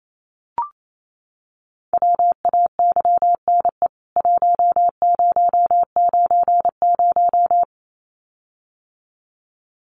Morse code, heard as a keyed tone, about 28 wpm, 700 Hz. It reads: WAYNE 1090
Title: CWT Exchange 28wpm